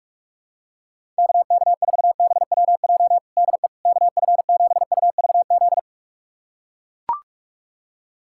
Morse code, heard as a keyed tone, about 41 wpm, 700 Hz. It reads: KK4BWJ DE KF7UVZ
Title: USA Callsign DE Callsign - 6 Characters - 45wpm